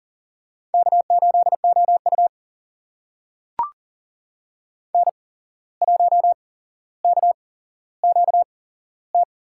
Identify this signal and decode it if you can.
Morse code, heard as a keyed tone, about 38 wpm, spaced as for 16 wpm, 700 Hz. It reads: K8OU N1KQT
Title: USA Call Signs - 8x Character Spacing - 40wpm